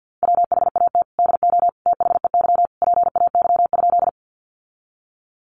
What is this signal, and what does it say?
Morse code, heard as a keyed tone, about 50 wpm, 700 Hz. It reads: WHAT DO THEY PAY?
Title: Sentences from Top 700 Words 50wpm